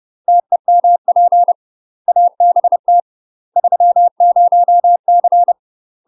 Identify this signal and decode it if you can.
Morse code, heard as a keyed tone, about 30 wpm, 700 Hz. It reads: TEMP ABT 30C
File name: UK MASTER QSOs TRAINING / 30WPM / UK MASTER QSOs 0015